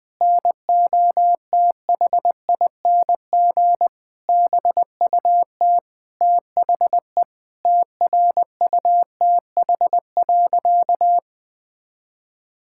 Morse code, heard as a keyed tone, about 21 wpm, 700 Hz. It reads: NOTHING BUT THE TRUTH.
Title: Sentences with 8-Word Phrases - Spoken Beginning - 20wpm